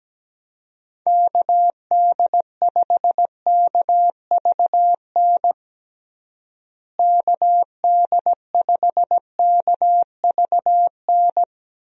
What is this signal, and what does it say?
Morse code, heard as a keyed tone, about 17 wpm, 700 Hz. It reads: KD5KVN KD5KVN
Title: USA Call Signs - 6 Characters - Repeated 3x - 17wpm